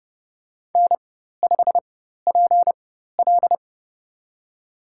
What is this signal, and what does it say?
Morse code, heard as a keyed tone, about 30 wpm, spaced as for 14 wpm, 700 Hz. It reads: N5PL
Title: USA Call Signs - 4 Characters - 4x Character Spacing - 30wpm